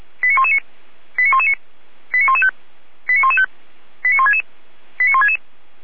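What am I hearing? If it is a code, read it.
Five-tone selective call (ZVEI), heard as quick runs of five tones, tones of about 80 ms. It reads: 89119 89119 89115 89115 89166 89166
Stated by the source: ZVEI five-tone call received by SDR, on the feuerwehr-orlamuende.org server (fire brigade Orlamünde).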